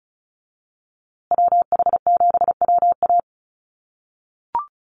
Morse code, heard as a keyed tone, about 34 wpm, 700 Hz. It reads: WH7WA
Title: USA Call Signs Repeated 2x 35wpm